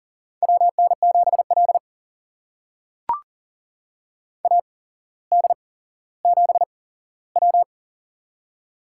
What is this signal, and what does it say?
Morse code, heard as a keyed tone, about 40 wpm, spaced as for 17 wpm, 700 Hz. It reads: WN7L AD7W
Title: USA Call Signs - 4 Characters - 8x Character Spacing - 40wpm